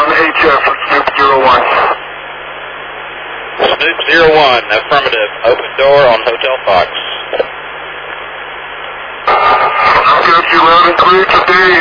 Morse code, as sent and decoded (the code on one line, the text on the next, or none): none